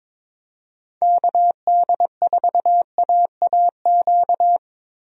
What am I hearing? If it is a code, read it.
Morse code, heard as a keyed tone, about 22 wpm, 700 Hz. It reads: KD4AAQ
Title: USA Call Signs - 6 Characters - Repeated 3x - 22wpm